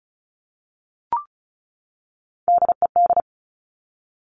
Morse code, heard as a keyed tone, about 36 wpm, 700 Hz. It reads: DED
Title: Top Three Letter Combinations in Words 35wpm